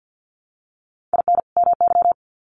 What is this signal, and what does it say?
Morse code, heard as a keyed tone, about 50 wpm, 700 Hz. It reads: IN MY
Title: Sets of 2 Words 50wpm